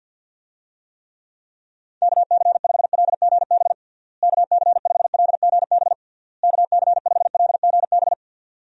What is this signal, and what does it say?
Morse code, heard as a keyed tone, about 51 wpm, 700 Hz. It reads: KC5LGB KC5LGB KC5LGB
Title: USA Call Signs - 6 Characters - Repeated 3x - 50wpm